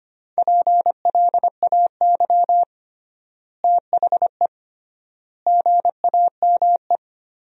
Morse code, heard as a keyed tone, about 25 wpm, 700 Hz. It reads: PLAY THE GAME
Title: 3x Word-Spacing - Sentences from Top 500 Words - 25wpm